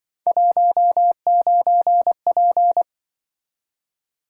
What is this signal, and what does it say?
Morse code, heard as a keyed tone, about 25 wpm, 700 Hz. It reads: 19P
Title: Speed-Racing - Three Letter-Number Combinations - 20wpm